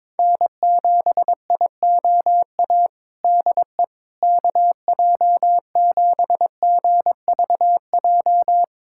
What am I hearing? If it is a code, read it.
Morse code, heard as a keyed tone, about 21 wpm, 700 Hz. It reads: N7IOA DE KJ7GVJ
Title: USA Callsign DE Callsign 22wpm